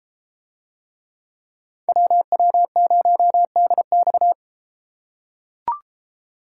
Morse code, heard as a keyed tone, about 32 wpm, 700 Hz. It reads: WW0DX